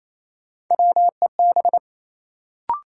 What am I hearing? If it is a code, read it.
Morse code, heard as a keyed tone, about 28 wpm, 700 Hz. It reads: WEB
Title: Sentences from Top 1000 Words - Review Part 4 - New Word Drill - 28wpm